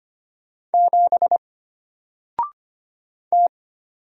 Morse code, heard as a keyed tone, about 25 wpm, 700 Hz. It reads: 7 T
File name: Single-Letter-Number-25wpm